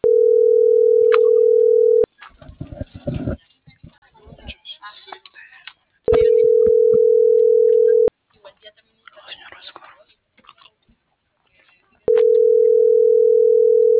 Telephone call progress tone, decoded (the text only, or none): ringback tone